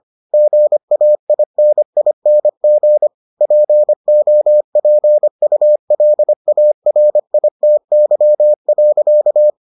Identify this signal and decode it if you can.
Morse code, heard as a keyed tone, about 25 wpm, 600 Hz. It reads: GAINING POPULARITY.